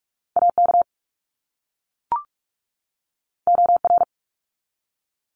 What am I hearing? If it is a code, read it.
Morse code, heard as a keyed tone, about 44 wpm, 700 Hz. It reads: AK OR
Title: ARRL Sections 45wpm